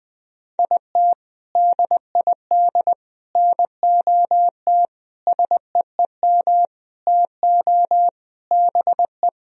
Morse code, heard as a keyed tone, about 20 wpm, 700 Hz. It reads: IT DID NOT SEEM TO BE
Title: Sentences from Top 1000 Words - Review Part 1 - 20wpm